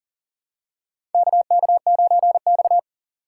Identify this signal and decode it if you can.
Morse code, heard as a keyed tone, about 40 wpm, 700 Hz. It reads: KK9X
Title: USA Call Signs - 4 Characters - 2x Character Spacing - 40wpm